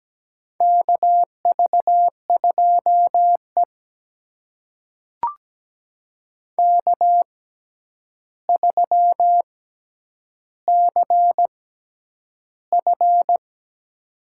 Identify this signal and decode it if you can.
Morse code, heard as a keyed tone, about 17 wpm, spaced as for 9 wpm, 700 Hz. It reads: KV2E K3CF
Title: USA Call Signs - 4 Characters - 6x Character Spacing - 17wpm